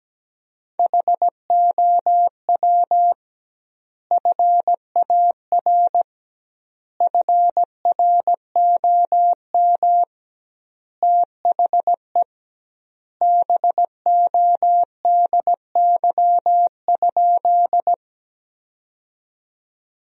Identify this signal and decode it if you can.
Morse code, heard as a keyed tone, about 17 wpm, 700 Hz. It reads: HOW FAR FROM THE BODY?
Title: 2x Word-Spacing - Sentences from Top 400 Words - 17wpm